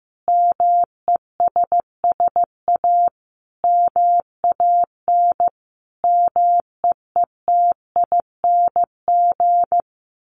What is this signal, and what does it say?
Morse code, heard as a keyed tone, about 15 wpm, 700 Hz. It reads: MESS A MAN MEETING
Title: A Sound Of Thunder 15wpm